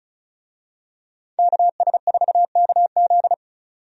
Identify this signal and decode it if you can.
Morse code, heard as a keyed tone, about 36 wpm, 700 Hz. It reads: KS4KZ
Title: USA Call Signs - 5 Characters - 35wpm